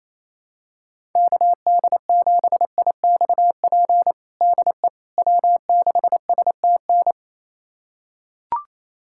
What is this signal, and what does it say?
Morse code, heard as a keyed tone, about 28 wpm, 700 Hz. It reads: KD7IXP DE W6STN
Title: USA Callsign DE Callsign 28wpm